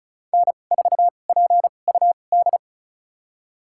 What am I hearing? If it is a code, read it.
Morse code, heard as a keyed tone, about 36 wpm, spaced as for 25 wpm, 700 Hz. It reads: N4PUD
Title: USA Call Signs - 5 Characters - 2x Character Spacing - 35wpm